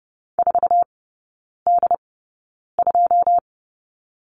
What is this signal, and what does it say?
Morse code, heard as a keyed tone, about 30 wpm, spaced as for 9 wpm, 700 Hz. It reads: 4D2
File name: 081 Lesson 14e - Number 2 - Three Characters 30wpm